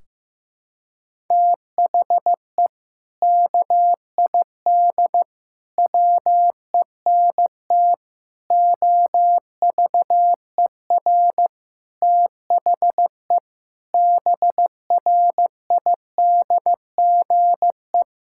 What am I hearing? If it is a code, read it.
Morse code, heard as a keyed tone, about 16 wpm, 700 Hz. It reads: THE KID WENT OVER THE BRIDGE
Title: Sentences from Top 1600 Words 15wpm